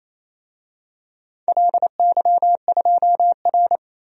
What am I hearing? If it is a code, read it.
Morse code, heard as a keyed tone, about 28 wpm, 700 Hz. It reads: LY2R
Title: DXCC-100 Call Signs - 4 Characters - Repeated 2x - 28wpm